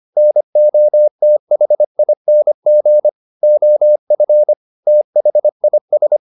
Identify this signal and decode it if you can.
Morse code, heard as a keyed tone, about 25 wpm, 600 Hz. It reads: NOTHING OF THIS
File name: the old man and the sea part 1 25wpm